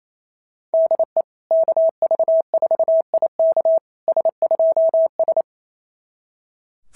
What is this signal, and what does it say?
Morse code, heard as a keyed tone, about 27 wpm, 660 Hz. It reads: DE KV4IK S2S
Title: SOTA QSO Practice 28wpm